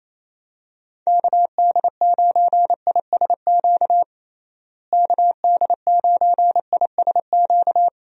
Morse code, heard as a keyed tone, about 28 wpm, 700 Hz. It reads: KD9ISQ KD9ISQ
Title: USA Call Signs - 6 Characters - Repeated 3x - 28wpm